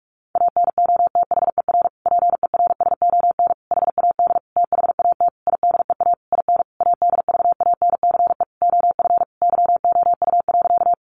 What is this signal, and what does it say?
Morse code, heard as a keyed tone, about 44 wpm, 700 Hz. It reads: ANOTHER PERSON HAD THAT IDEA IN ADVANCE OF YOU.
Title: Sentences from Top 1400 Words 45wpm